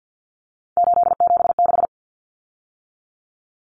Morse code, heard as a keyed tone, about 50 wpm, 700 Hz. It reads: 876
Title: Common Cut Numbers - 3-Digit 50wpm